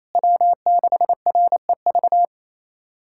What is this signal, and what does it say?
Morse code, heard as a keyed tone, about 28 wpm, 700 Hz. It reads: W6REV
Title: USA Callsign DE Callsign 28wpm